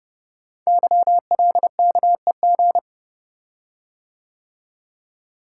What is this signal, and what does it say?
Morse code, heard as a keyed tone, about 30 wpm, 700 Hz. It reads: YLKEG
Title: Code Groups - 5 Letters - 30wpm